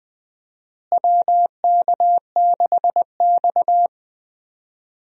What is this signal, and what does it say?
Morse code, heard as a keyed tone, about 20 wpm, 700 Hz. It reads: WK6X
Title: USA Call Signs - 4 Characters - Repeated 2x - 20wpm